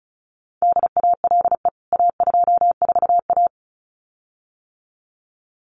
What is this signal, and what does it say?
Morse code, heard as a keyed tone, about 34 wpm, 700 Hz. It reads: DALE A24A
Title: CWT Exchange 35wpm